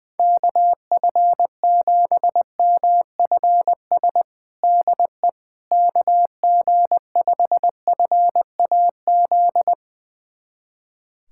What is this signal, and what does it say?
Morse code, heard as a keyed tone, about 20 wpm, 700 Hz. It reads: KF7MFS DE KG5FAZ
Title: USA Callsign DE Callsign - 6 Characters - 20wpm